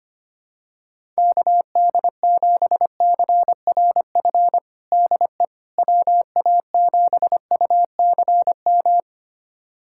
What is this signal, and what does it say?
Morse code, heard as a keyed tone, about 25 wpm, 700 Hz. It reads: KD7CRF DE WA7UCM
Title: USA Callsign DE Callsign - 6 Characters - 25wpm